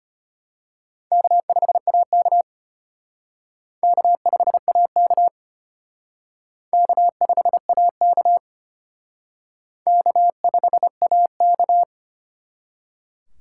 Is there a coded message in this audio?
Yes: Morse code from 1.1 to 11.8 s, heard as a keyed tone, about 32 wpm, 700 Hz. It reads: K5AK K5AK K5AK K5AK